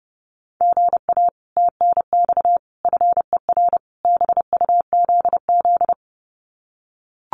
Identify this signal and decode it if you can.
Morse code, heard as a keyed tone, about 30 wpm, 700 Hz. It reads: GA TNX FER BUZZ
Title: Common QSO Phrases Encore 30wpm